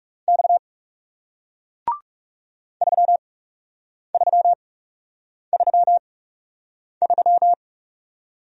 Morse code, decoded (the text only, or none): X 3333